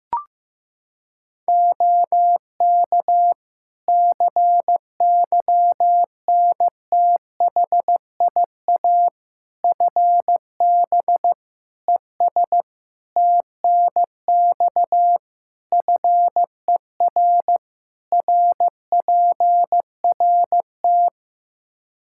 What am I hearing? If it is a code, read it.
Morse code, heard as a keyed tone, about 15 wpm, 700 Hz. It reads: OK CYNTHIA FB ES TNX FER RPRT
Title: K1USN SST QSO Practice 15wpm